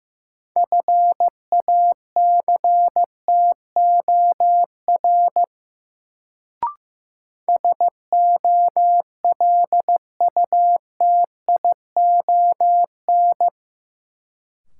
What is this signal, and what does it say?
Morse code, heard as a keyed tone, about 15 wpm, 700 Hz. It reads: FACTOR SOLUTION
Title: Sentences from Top 1300 Words - New Word Drill Alt - 15wpm